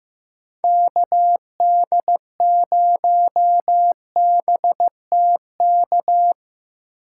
Morse code, heard as a keyed tone, about 17 wpm, 700 Hz. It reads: KD0BTK
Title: Speed Racing - USA 2x3 Call Signs 15wpm